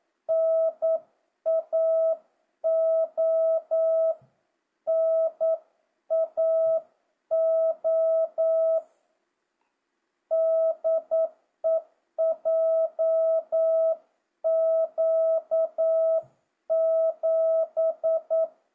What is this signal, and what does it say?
Morse code, heard as a keyed tone, about 9 wpm, 650 Hz. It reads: NAO NAO DE JQ7
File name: jh8cbh / 2025cw